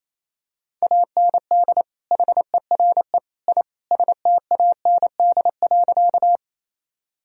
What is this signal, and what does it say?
Morse code, heard as a keyed tone, about 27 wpm, 700 Hz. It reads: AND HERE I STAND.
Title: Sentences from Top 1000 Words - Review Part 1 - 28wpm